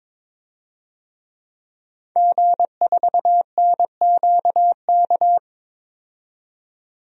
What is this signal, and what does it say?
Morse code, heard as a keyed tone, about 21 wpm, 700 Hz. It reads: G4NQK
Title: DXCC-100 Call Signs - 5 Characters - 22wpm